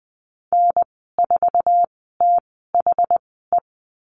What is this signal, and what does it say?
Morse code, heard as a keyed tone, about 20 wpm, spaced as for 14 wpm, 700 Hz. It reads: N4THE
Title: 062 Lesson 11g - Letter h - Call Signs 20wpm